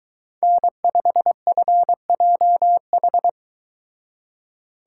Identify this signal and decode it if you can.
Morse code, heard as a keyed tone, about 23 wpm, 700 Hz. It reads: N5FJH